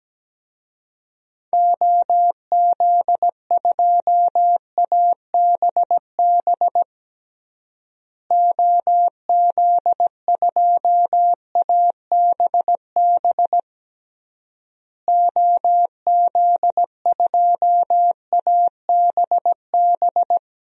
Morse code, heard as a keyed tone, about 17 wpm, 700 Hz. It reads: OZ2ABB OZ2ABB OZ2ABB
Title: DXCC-100 Call Signs - 6 Characters - Repeated 3x - 17wpm